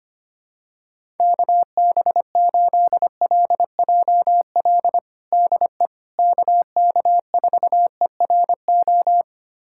Morse code, heard as a keyed tone, about 25 wpm, 700 Hz. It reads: KB8LJL DE KK4ERO
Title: USA Callsign DE Callsign 25wpm